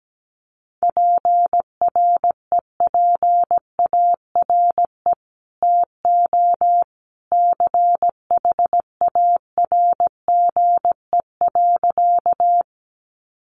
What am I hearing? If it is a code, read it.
Morse code, heard as a keyed tone, about 17 wpm, 700 Hz. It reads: PREPARE TO CHARGE.
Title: Sentences from Top 900 Words 17wpm